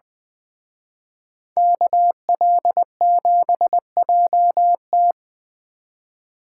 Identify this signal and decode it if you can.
Morse code, heard as a keyed tone, about 24 wpm, 700 Hz. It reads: KL7JT
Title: Speed Racing - USA 2x2 Call Signs 20wpm